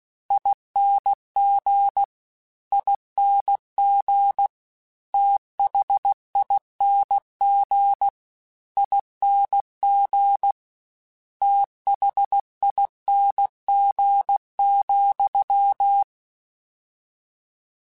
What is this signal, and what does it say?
Morse code, heard as a keyed tone, about 16 wpm, 800 Hz. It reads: ING ING THING ING THING,
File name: Special INGs / Special INGs-mp3s / VerySlowToFaster Ing @15